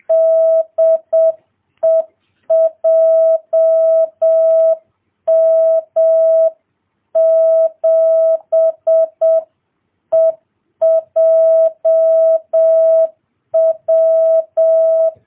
Morse code, heard as a keyed tone, about 7 wpm, 650 Hz. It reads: DE JM7EJW